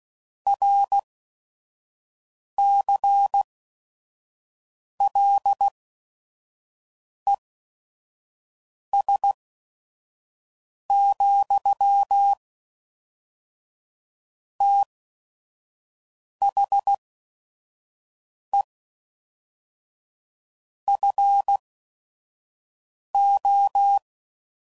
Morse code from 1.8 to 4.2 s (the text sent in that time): C